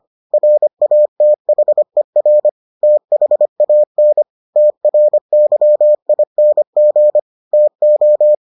Morse code, heard as a keyed tone, about 25 wpm, 600 Hz. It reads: RATHER THAN TRYING TO